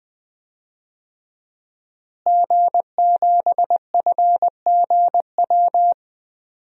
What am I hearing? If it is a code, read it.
Morse code, heard as a keyed tone, about 20 wpm, 700 Hz. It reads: G7FGW
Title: DXCC-100 Call Signs - 5 Characters - Repeated 2x - 20wpm